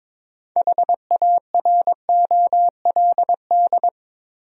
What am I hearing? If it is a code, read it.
Morse code, heard as a keyed tone, about 22 wpm, 700 Hz. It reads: HAROLD
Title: USA Common Names - 22wpm